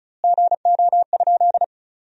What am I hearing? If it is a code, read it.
Morse code, heard as a keyed tone, about 34 wpm, 700 Hz. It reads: GO?
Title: 2x Word-Spacing - Sentences from Top 200 Words - 35wpm